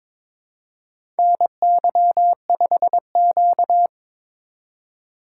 Morse code, heard as a keyed tone, about 22 wpm, 700 Hz. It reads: NY5Q
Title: USA Call Signs - 4 Characters - 2x Character Spacing - 22wpm